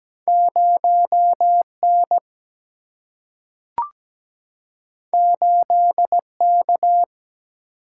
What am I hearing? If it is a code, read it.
Morse code, heard as a keyed tone, about 17 wpm, 700 Hz. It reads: 0N 8K
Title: Single-Letter Single-Number 17wpm